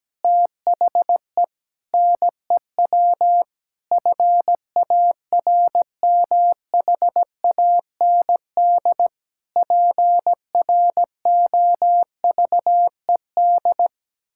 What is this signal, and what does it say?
Morse code, heard as a keyed tone, about 17 wpm, 700 Hz. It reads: THE NEW FARMHAND PROVED
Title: Pangram Challenge v2 - 17wpm